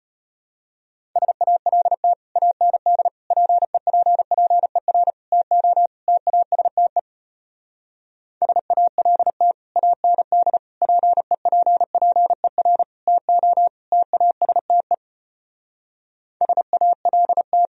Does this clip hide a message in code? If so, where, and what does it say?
Yes: Morse code from 1.2 to 17.7 s, heard as a keyed tone, about 33 wpm, 700 Hz. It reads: SALT AND PEPPER TO TASTE SALT AND PEPPER TO TASTE SALT